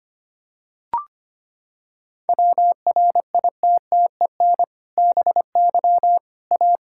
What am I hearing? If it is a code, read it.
Morse code, heard as a keyed tone, about 25 wpm, 700 Hz. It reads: WRITTEN BY A
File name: practice / 25wpm / Sentences with 2-Word Phrases - 25wpm